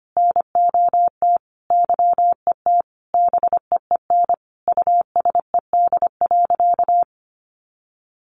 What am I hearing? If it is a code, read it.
Morse code, heard as a keyed tone, about 25 wpm, 700 Hz. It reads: NOT YET BEEN USED.
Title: Sentences from Top 700 Words 25wpm